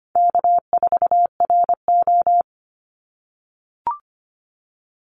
Morse code, heard as a keyed tone, about 25 wpm, 700 Hz. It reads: K4RO
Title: Easy Call Signs - 25wpm